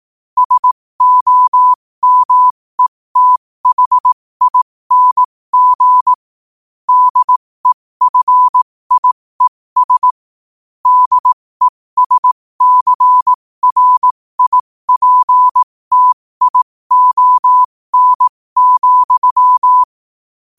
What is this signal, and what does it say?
Morse code, heard as a keyed tone, about 18 wpm, spaced as for 15 wpm, 1000 Hz. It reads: SOMETHING DEFIES DESCRIPTION,
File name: qotdfiles / 15wpm